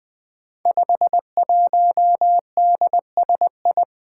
Morse code, heard as a keyed tone, about 20 wpm, 700 Hz. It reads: 51DSI
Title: Five Challenging Letter-Number Combinations 20wpm